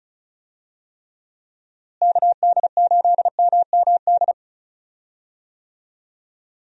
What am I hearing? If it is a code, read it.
Morse code, heard as a keyed tone, about 36 wpm, 700 Hz. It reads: KD8MMD
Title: USA Call Signs - 6 Characters - Repeated 3x - 35wpm